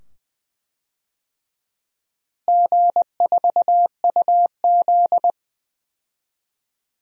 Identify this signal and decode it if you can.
Morse code, heard as a keyed tone, about 20 wpm, 700 Hz. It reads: G4UZ
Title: DXCC-100 Call Signs - 4 Characters - Repeated 2x - 20wpm